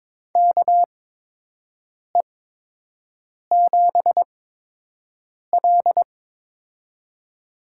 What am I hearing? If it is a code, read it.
Morse code, heard as a keyed tone, about 22 wpm, spaced as for 6 wpm, 700 Hz. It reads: KE7L
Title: USA Call Signs - 4 Characters - 8x Character Spacing - 22wpm